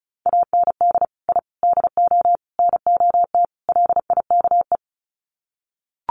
Morse code, heard as a keyed tone, about 34 wpm, 700 Hz. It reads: AND I DO NOT LIKE
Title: Sets of 5 Words Encore2 35wpm